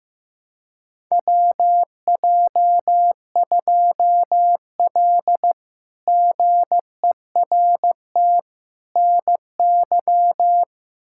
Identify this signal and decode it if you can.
Morse code, heard as a keyed tone, about 15 wpm, 700 Hz. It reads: WJ2L GERT NY
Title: K1USN SST QSO Practice - 15wpm